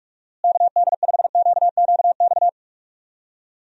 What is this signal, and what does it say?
Morse code, heard as a keyed tone, about 42 wpm, 700 Hz. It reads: KD5QQX
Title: USA Call Signs - 6 Characters - Repeated 2x - 45wpm